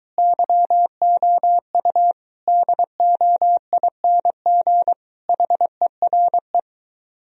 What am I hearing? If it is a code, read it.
Morse code, heard as a keyed tone, about 24 wpm, 700 Hz. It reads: YOU DOING HERE